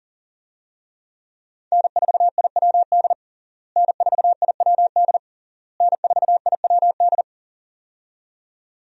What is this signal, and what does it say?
Morse code, heard as a keyed tone, about 40 wpm, 690 Hz. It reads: N4IWD N4IWD N4IWD
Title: USA Call Signs - 5 Characters - Repeated 3x - 40wpm